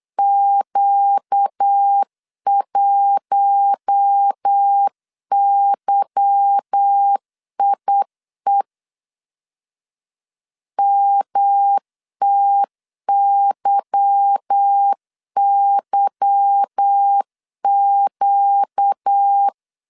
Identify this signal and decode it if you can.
Morse code, heard as a keyed tone, about 8 wpm, 800 Hz. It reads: Q1YIE MTYYQ
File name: LEZIONI / 8 YJ45 / 8 Y 8wpm 3245